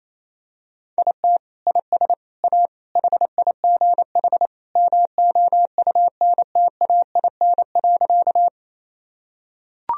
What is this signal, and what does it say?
Morse code, heard as a keyed tone, about 28 wpm, 700 Hz. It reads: IT IS A HIGH MOUNTAIN.
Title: Sentences from Top 1000 Words - Review Part 1 - 28wpm